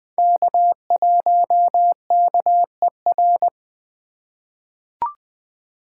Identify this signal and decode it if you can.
Morse code, heard as a keyed tone, about 20 wpm, 700 Hz. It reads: K1KER
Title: USA Call Signs - 5 Characters - 2x Character Spacing - 20wpm